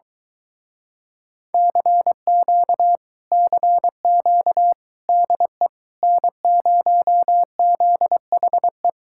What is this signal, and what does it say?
Morse code, heard as a keyed tone, about 22 wpm, 700 Hz. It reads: CQ CQ DE N0ZHE